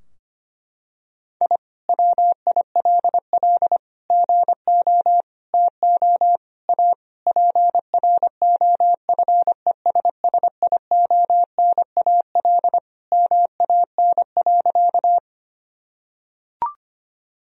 Morse code, heard as a keyed tone, about 25 wpm, 700 Hz. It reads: I WILL GO TO A PROFESSIONAL MAN.
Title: Sentences from Top 1000 Words - Review Part 4 - 25wpm